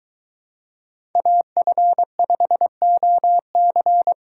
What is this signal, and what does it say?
Morse code, heard as a keyed tone, about 22 wpm, 700 Hz. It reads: AF5OC